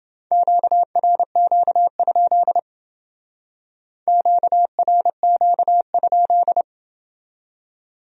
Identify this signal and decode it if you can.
Morse code, heard as a keyed tone, about 29 wpm, 700 Hz. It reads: QRQ? QRQ?